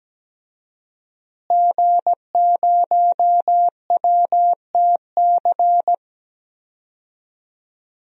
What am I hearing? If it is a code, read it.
Morse code, heard as a keyed tone, about 17 wpm, 700 Hz. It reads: G0WTC
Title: DXCC-100 Call Signs - 5 Characters - Repeated 2x - 17wpm